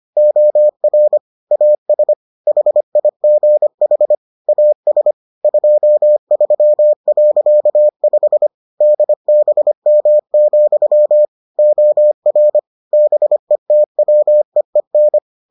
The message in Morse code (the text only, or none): OR AS HIGH AS 23.5 DBM, OR BETWEEN